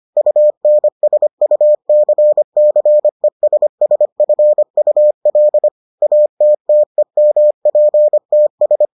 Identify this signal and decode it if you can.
Morse code, heard as a keyed tone, about 25 wpm, 600 Hz. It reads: UNSUCCESSFUL ATTEMPTS